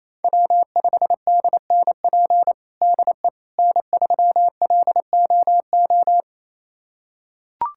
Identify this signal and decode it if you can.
Morse code, heard as a keyed tone, about 28 wpm, 700 Hz. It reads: W5DNP DE N3LOO
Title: USA Callsign DE Callsign - 5 Characters - 28wpm